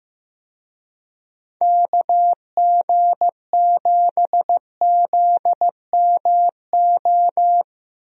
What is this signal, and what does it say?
Morse code, heard as a keyed tone, about 15 wpm, 700 Hz. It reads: KG7ZMO
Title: USA Call Signs - 6 Characters - 15wpm